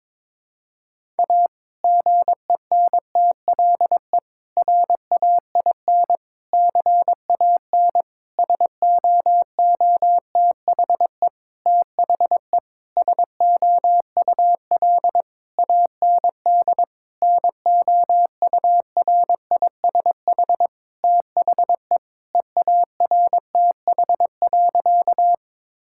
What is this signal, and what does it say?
Morse code, heard as a keyed tone, about 22 wpm, 700 Hz. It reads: A GENTLE RAIN CAN SOOTHE THE SOUL AND NOURISH THE EARTH.
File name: Difficult Sentences with Hints - 22wpm